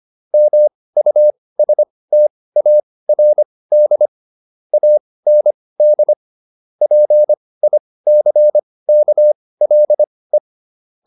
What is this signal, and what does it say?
Morse code, heard as a keyed tone, about 25 wpm, spaced as for 18 wpm, 600 Hz. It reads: MUSTARD AND PICKLE